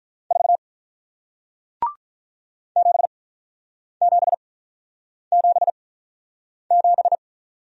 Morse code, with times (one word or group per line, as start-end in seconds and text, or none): 0.3-0.6 4
2.8-7.2 7777